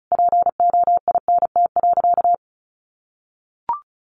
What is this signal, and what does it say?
Morse code, heard as a keyed tone, about 34 wpm, 700 Hz. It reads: POINT.
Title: Sentences from Top 200 Words 35wpm